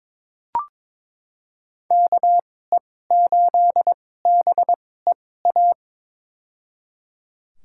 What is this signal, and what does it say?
Morse code, heard as a keyed tone, about 22 wpm, spaced as for 16 wpm, 700 Hz. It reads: KE8BEA
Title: USA Call Signs - 2x Character Spacing - 22wpm